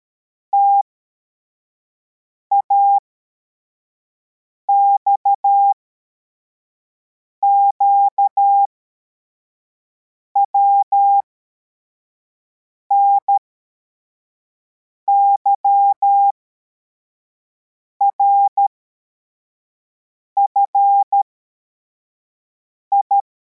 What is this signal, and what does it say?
Morse code, heard as a keyed tone, about 13 wpm, spaced as for 4 wpm, 800 Hz. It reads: TAXQW NYRFI